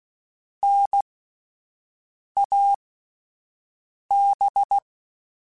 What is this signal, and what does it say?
Morse code, heard as a keyed tone, about 16 wpm, spaced as for 5 wpm, 800 Hz. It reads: NAB